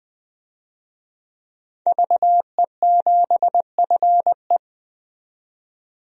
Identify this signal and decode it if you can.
Morse code, heard as a keyed tone, about 20 wpm, 700 Hz. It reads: VE7FE
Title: DXCC-100 Call Signs - 5 Characters - 20wpm